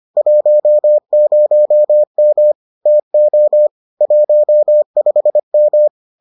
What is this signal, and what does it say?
Morse code, heard as a keyed tone, about 25 wpm, 600 Hz. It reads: 10M TO 15M